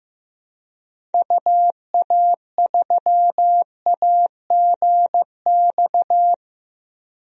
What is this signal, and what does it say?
Morse code, heard as a keyed tone, about 15 wpm, 700 Hz. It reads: UA3AGX
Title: DXCC-100 Call Signs - 6 Characters - Repeated 3x - 15wpm